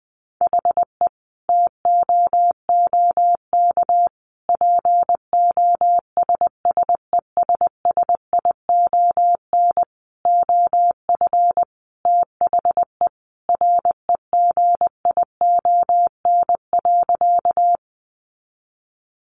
Morse code, heard as a keyed tone, about 20 wpm, 700 Hz. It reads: HE TOOK POSSESSION OF THE REGION.
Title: Sentences from Top 1500 Words 20wpm